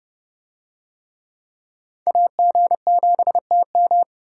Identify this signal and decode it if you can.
Morse code, heard as a keyed tone, about 27 wpm, 700 Hz. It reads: AG7TM